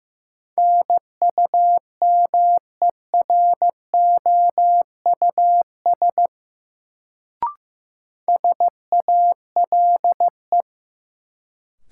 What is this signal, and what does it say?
Morse code, heard as a keyed tone, about 15 wpm, 700 Hz. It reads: NUMEROUS SALE
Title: Sentences from Top 1400 Words - New Word Drill - 15wpm